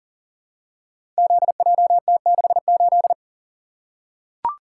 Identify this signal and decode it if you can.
Morse code, heard as a keyed tone, about 40 wpm, 700 Hz. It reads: ZJT68
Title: Code Groups - 5 Letter:Numbers - 40wpm